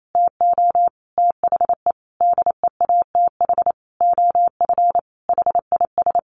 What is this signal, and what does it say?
Morse code, heard as a keyed tone, about 28 wpm, 700 Hz. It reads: TO THE DEATH OF HIS
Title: Sentences from Top 1500 Words 28wpm